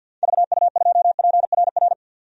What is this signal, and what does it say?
Morse code, heard as a keyed tone, about 49 wpm, 700 Hz. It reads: VU2PRR
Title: DXCC-100 Call Signs - 6 Characters - 50wpm